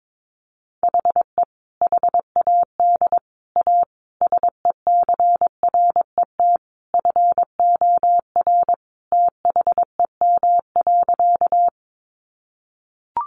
Morse code, heard as a keyed tone, about 22 wpm, 700 Hz. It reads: HE HAD A SECRET FOR THEM.
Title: Sentences from Top 1200 Words 22wpm